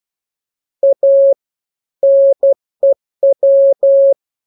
Morse code, heard as a keyed tone, about 12 wpm, 550 Hz. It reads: A NEW
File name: jokes / joke 16 12 0000